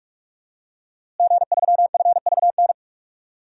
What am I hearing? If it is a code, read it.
Morse code, heard as a keyed tone, about 46 wpm, 700 Hz. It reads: G3FVN
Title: DXCC-100 Call Signs - 5 Characters - Repeated 3x - 45wpm